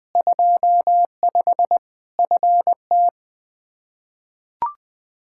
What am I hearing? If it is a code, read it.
Morse code, heard as a keyed tone, about 20 wpm, 700 Hz. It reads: 25 FT
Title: Practice QSO - Exchange 2 - Info - 20wpm